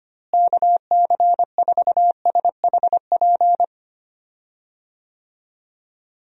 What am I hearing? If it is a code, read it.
Morse code, heard as a keyed tone, about 25 wpm, 700 Hz. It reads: KC4SHP
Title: USA Call Signs - 6 Characters - Repeated 3x - 25wpm